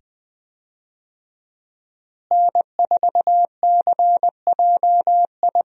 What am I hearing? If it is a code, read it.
Morse code, heard as a keyed tone, about 20 wpm, 700 Hz. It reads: N4CJI
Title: USA Call Signs - 5 Characters - Repeated 2x - 20wpm